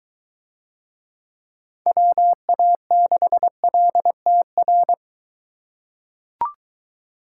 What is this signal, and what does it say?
Morse code, heard as a keyed tone, about 21 wpm, 700 Hz. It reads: WA6LTR